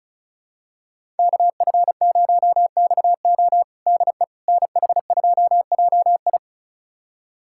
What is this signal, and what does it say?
Morse code, heard as a keyed tone, about 36 wpm, 700 Hz. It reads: KF0XO DE NH2JI
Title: USA Callsign DE Callsign - 5 Characters - 35wpm